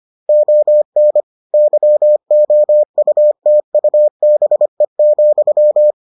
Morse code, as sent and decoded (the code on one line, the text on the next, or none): --- -. / -.-- --- ..- - ..- -... --..--
ON YOUTUBE,